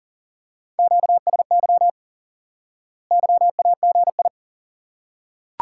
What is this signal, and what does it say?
Morse code, heard as a keyed tone, about 38 wpm, 700 Hz. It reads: QSY YAGI
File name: IQE Course 3 - Lesson 89 - YAGI - Two Elements - 40wpm